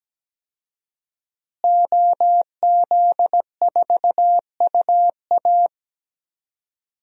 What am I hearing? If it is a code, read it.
Morse code, heard as a keyed tone, about 17 wpm, 700 Hz. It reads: OZ4UA
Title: DXCC-100 Call Signs - 5 Characters - Repeated 2x - 17wpm